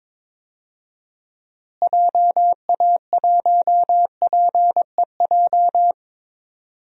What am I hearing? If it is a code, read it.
Morse code, heard as a keyed tone, about 22 wpm, 700 Hz. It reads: JA1PEJ
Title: DXCC-100 Call Signs - 6 Characters - Repeated 3x - 22wpm